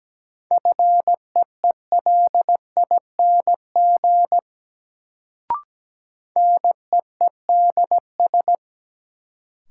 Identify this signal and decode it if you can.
Morse code, heard as a keyed tone, about 17 wpm, 700 Hz. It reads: FEELING NEEDS
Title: Sentences from Top 1000 Words - Review Part 3 - New Word Drill - 17wpm